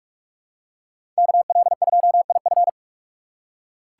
Morse code, heard as a keyed tone, about 46 wpm, 700 Hz. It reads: KL2IF
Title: USA Call Signs - 4x Character Spacing - 45wpm